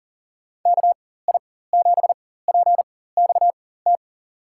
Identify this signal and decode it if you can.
Morse code, heard as a keyed tone, about 40 wpm, spaced as for 19 wpm, 700 Hz. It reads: KI7PXT
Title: USA Call Signs - 6 Characters - 4x Character Spacing - 40wpm